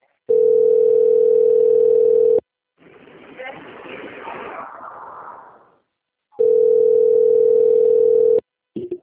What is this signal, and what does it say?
Ringback tone (North America), 440 + 480 Hz.
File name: out- - - - - .4494